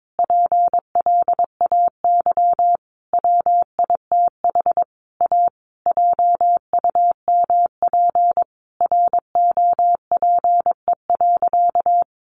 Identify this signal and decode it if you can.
Morse code, heard as a keyed tone, about 22 wpm, 700 Hz. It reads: PLAY WITH A JUMP ROPE.
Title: Easy Phrases 22wpm